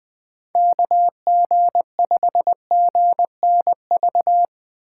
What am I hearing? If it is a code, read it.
Morse code, heard as a keyed tone, about 20 wpm, 700 Hz. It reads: KG5GNV
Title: USA Call Signs - 6 Characters - 6x Character Spacing - 20wpm